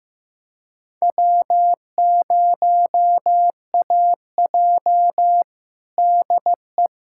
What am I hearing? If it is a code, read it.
Morse code, heard as a keyed tone, about 16 wpm, 700 Hz. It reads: W0AJ DE
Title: USA Callsign DE Callsign - 4 Characters - 15wpm